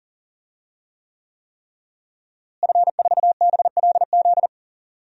Morse code, heard as a keyed tone, about 40 wpm, 700 Hz. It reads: F4BLZ
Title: DXCC-100 Call Signs - 5 Characters - Repeated 3x - 40wpm